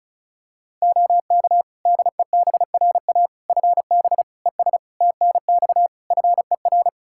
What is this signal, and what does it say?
Morse code, heard as a keyed tone, about 36 wpm, 700 Hz. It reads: OK DEBRA FB ES TNX FER
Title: Practice QSO - Exchange 2 - Info - 35wpm